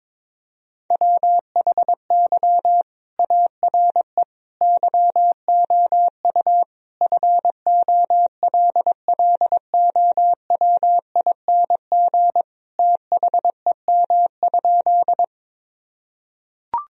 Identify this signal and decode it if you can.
Morse code, heard as a keyed tone, about 22 wpm, 700 Hz. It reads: WHY ARE YOU FOLLOWING THEM?
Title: Sentences from Top 1000 Words - Review Part 4 - 22wpm